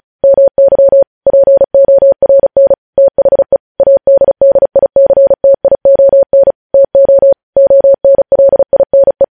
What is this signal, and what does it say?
Morse code, heard as a keyed tone, about 36 wpm, 550 Hz. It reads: MY PORN THE ADDICTION TO ONLINE